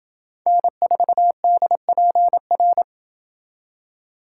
Morse code, heard as a keyed tone, about 28 wpm, 700 Hz. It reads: N4DPR